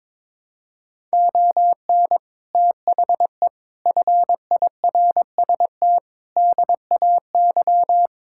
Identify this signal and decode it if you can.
Morse code, heard as a keyed tone, about 22 wpm, 700 Hz. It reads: ON THE FIRST DAY
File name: practice / 22wpm / Sets of 6 Words - 22wpm